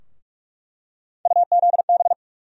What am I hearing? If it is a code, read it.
Morse code, heard as a keyed tone, about 46 wpm, 700 Hz. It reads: UZB
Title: USA Call Sign Postfixes - 3 Letters 45wpm